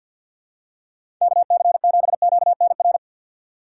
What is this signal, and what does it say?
Morse code, heard as a keyed tone, about 53 wpm, 700 Hz. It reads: KC7QNR
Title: USA Call Signs - 6 Characters - 3x Character Spacing - 50wpm